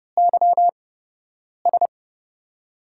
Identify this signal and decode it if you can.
Morse code, heard as a keyed tone, about 30 wpm, spaced as for 8 wpm, 700 Hz. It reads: YS
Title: USA Call Signs - 6 Characters - 8x Character Spacing - 30wpm